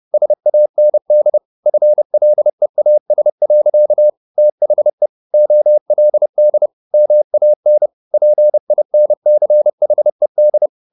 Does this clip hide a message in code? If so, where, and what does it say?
Yes: Morse code from 0.1 to 10.7 s, heard as a keyed tone, about 30 wpm, 600 Hz. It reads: SAND FLEAS. THE OLD MAN PINCHED